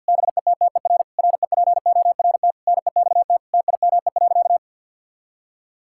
Morse code, heard as a keyed tone, about 52 wpm, 700 Hz. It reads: BETTER REPORT NEXT TIME.
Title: Sentences from Top 1000 Words - Review Part 6 - 50wpm